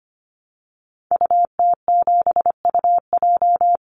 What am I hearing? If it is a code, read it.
Morse code, heard as a keyed tone, about 25 wpm, 700 Hz. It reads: UT7UJ
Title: Call Signs Repeated 2x - 25wpm